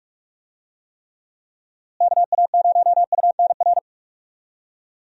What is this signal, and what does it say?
Morse code, heard as a keyed tone, about 46 wpm, 700 Hz. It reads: KA0UNR
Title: USA Call Signs - 6 Characters - Repeated 2x - 45wpm